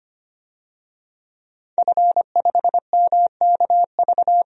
Morse code, heard as a keyed tone, about 25 wpm, 700 Hz. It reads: F5MKV
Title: DXCC-100 Call Signs - 5 Characters - Repeated 3x - 25wpm